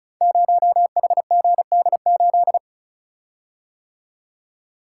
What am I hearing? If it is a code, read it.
Morse code, heard as a keyed tone, about 36 wpm, 700 Hz. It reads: 0HGD8
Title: Code Groups - 5 Letter:Numbers - 35wpm